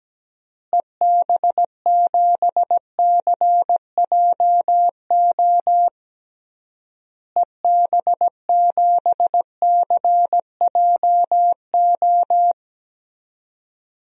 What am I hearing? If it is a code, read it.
Morse code, heard as a keyed tone, about 17 wpm, 700 Hz. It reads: EB7CJO EB7CJO
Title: DXCC-100 Call Signs - 6 Characters - Repeated 3x - 17wpm